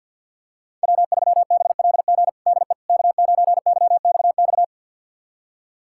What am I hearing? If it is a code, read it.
Morse code, heard as a keyed tone, about 48 wpm, 700 Hz. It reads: W3BLG DE K9YXX
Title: USA Callsign DE Callsign - 5 Characters - 50wpm